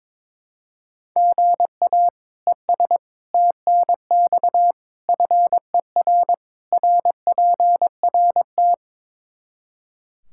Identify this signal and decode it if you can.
Morse code, heard as a keyed tone, about 22 wpm, 700 Hz. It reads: GA ES TNX FER RPRT
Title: Practice QSO - Exchange 1 - Beginning - 22wpm